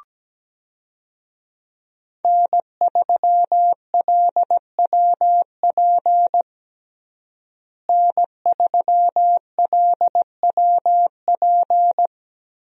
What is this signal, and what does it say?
Morse code, heard as a keyed tone, about 17 wpm, 700 Hz. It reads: N3LWP N3LWP
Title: USA Call Signs - 5 Characters - Repeated 2x - 17wpm